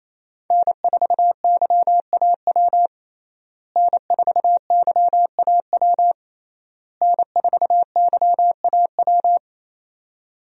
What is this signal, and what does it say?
Morse code, heard as a keyed tone, about 28 wpm, 700 Hz. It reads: N4YAW N4YAW N4YAW
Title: USA Call Signs - 5 Characters - Repeated 3x - 28wpm